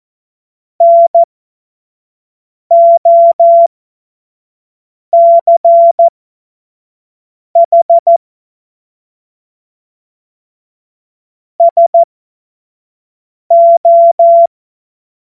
Morse code, heard as a keyed tone, about 14 wpm, spaced as for 5 wpm, 680 Hz. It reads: NOCH SO
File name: Morsekurs / Witze / Witz 21. Februar 2023 25 F70